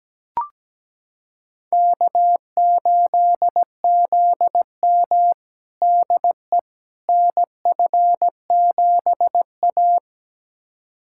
Morse code, heard as a keyed tone, about 17 wpm, 700 Hz. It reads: K8ZM DE NF7A
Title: USA Callsign DE Callsign - 4 Characters - 17wpm